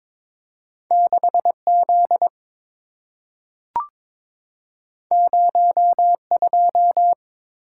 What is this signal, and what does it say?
Morse code, heard as a keyed tone, about 22 wpm, 690 Hz. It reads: 6Z 02
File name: practice / 22wpm / Two Letter-Number Combinations - 22wpm